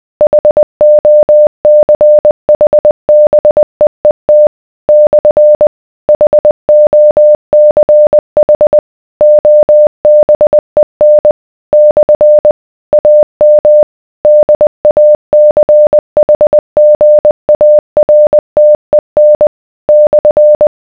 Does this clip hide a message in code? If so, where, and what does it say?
Yes: Morse code from 0.2 to 20.7 s, heard as a keyed tone, about 20 wpm, 600 Hz. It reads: HOCHBEET / HOCH OBEN / AM DACHGARTEN /